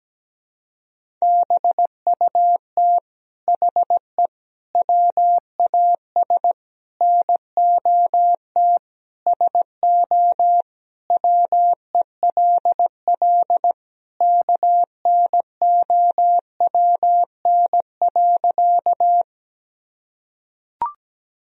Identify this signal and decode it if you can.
Morse code, heard as a keyed tone, about 17 wpm, 700 Hz. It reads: BUT HE WAS NOT SO WELL KNOWN.
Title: Sentences from Top 1000 Words - Review Part 3 - 17wpm